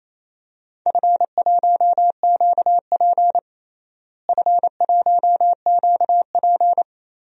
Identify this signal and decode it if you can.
Morse code, heard as a keyed tone, about 28 wpm, 700 Hz. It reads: F1QP F1QP
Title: DXCC-100 Call Signs - 4 Characters - Repeated 2x - 28wpm